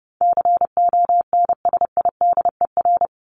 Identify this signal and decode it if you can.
Morse code, heard as a keyed tone, about 30 wpm, 700 Hz. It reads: CONSIDER
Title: ING Suffix - Top 100 Words with Base Words 30wpm